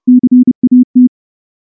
Morse code, heard as a keyed tone, about 29 wpm, 250 Hz. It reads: CAT
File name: morse-training-30WPM